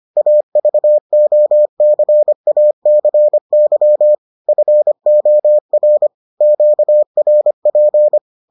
Morse code, heard as a keyed tone, about 25 wpm, 600 Hz. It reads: AVOCACY FOR QRP